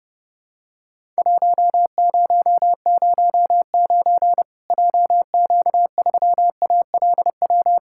Morse code, heard as a keyed tone, about 30 wpm, 700 Hz. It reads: 1009 JQ3ALW
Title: Worked All Europe DX Contest - QTC Exchange - 30wpm